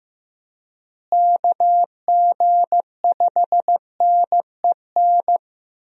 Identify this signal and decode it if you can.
Morse code, heard as a keyed tone, about 15 wpm, 700 Hz. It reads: KG5NEN
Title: USA Call Signs - 6 Characters - Repeated 2x - 15wpm